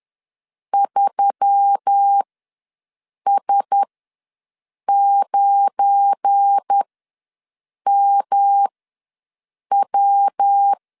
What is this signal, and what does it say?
Morse code, heard as a keyed tone, about 10 wpm, spaced as for 6 wpm, 800 Hz. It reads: 3S9MW
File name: LEZIONI / 7 GF9Z / 7 GF9 ws 10wpm 4034